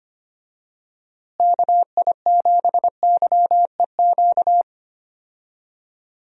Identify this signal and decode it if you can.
Morse code, heard as a keyed tone, about 25 wpm, 700 Hz. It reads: KI7YEQ
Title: USA Call Signs - 6 Characters - Repeated 3x - 25wpm